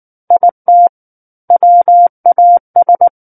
Morse code, heard as a keyed tone, about 19 wpm, 700 Hz. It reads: IT WAS
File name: Book One / 10-Chapter 10 - 19 WPM